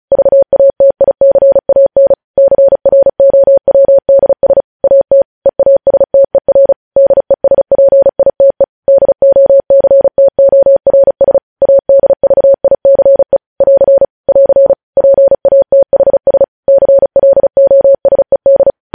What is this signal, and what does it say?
Morse code, heard as a keyed tone, about 36 wpm, 550 Hz. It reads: VATICAN CROWDS AT EASTER DESPITE DOCTORS ADVICE <AR> <AR> PATHS CLOSED
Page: news headlines in morse code at 35 wpm